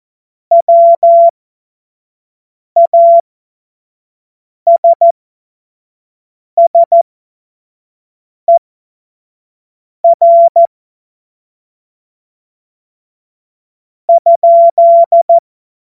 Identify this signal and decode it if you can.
Morse code, heard as a keyed tone, about 14 wpm, spaced as for 5 wpm, 680 Hz. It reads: WASSER ?
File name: Morsekurs / Witze / Witz 28. September 2021, 25 F70